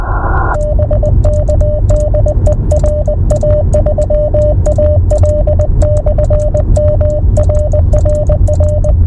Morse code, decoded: BK DE RA3AL/M RRR